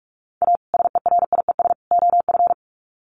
Morse code, heard as a keyed tone, about 44 wpm, 700 Hz. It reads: A SERIES OF
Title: Sets of 3 Words 45wpm